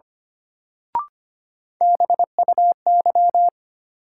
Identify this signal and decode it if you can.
Morse code, heard as a keyed tone, about 25 wpm, 700 Hz. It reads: BUY
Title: Sentences from Top 700 Words - New Word Drill - 25wpm